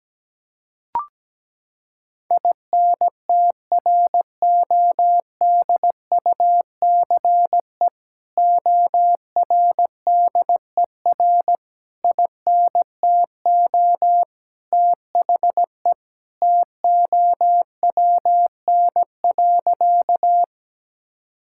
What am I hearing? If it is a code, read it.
Morse code, heard as a keyed tone, about 17 wpm, 700 Hz. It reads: INTRODUCE ORDER INTO THE TOWN.
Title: Sentences from Top 1600 Words 17wpm